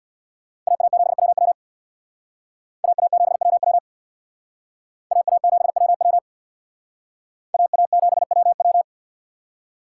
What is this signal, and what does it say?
Morse code, heard as a keyed tone, about 66 wpm, 700 Hz. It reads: AA7WW AA7WW AA7WW AA7WW